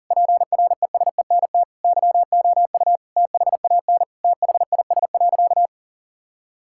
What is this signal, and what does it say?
Morse code, heard as a keyed tone, about 37 wpm, 700 Hz. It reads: PRESENT YOU THAN THIS.
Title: Sentences from Top 1000 Words - Review Part 6 - 40wpm